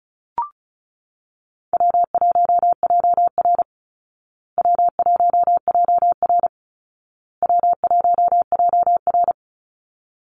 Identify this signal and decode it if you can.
Morse code, heard as a keyed tone, about 34 wpm, 700 Hz. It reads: W1JR W1JR W1JR
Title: Easy Call Signs Repeated 3x - 35wpm